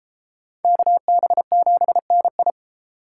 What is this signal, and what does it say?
Morse code, heard as a keyed tone, about 31 wpm, 700 Hz. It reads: KB7NI